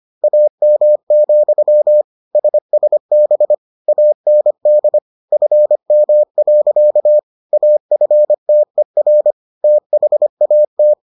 Morse code, heard as a keyed tone, about 25 wpm, 600 Hz. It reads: AM, SSB AND FM. AFTER THAT